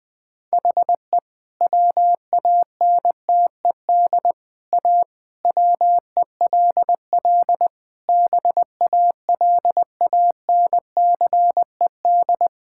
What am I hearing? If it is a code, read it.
Morse code, heard as a keyed tone, about 21 wpm, 700 Hz. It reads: HE WANTED A WELL BALANCED
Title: All-Age Appropriate Jokes - 20wpm